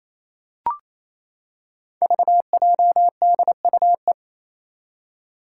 Morse code, heard as a keyed tone, about 28 wpm, 700 Hz. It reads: VJDUE
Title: Five Challenging Letter-Number Combinations 28wpm